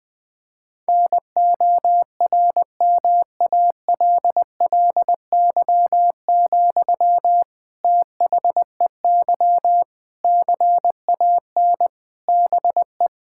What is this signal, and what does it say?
Morse code, heard as a keyed tone, about 20 wpm, 700 Hz. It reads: NORMALLY, THEY CAN BE
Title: Sentences from Top 1000 Words - Review Part 6 - 20wpm